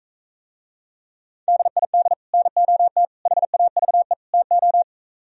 Morse code, heard as a keyed tone, about 43 wpm, 700 Hz. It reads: DID NOT HAVE TO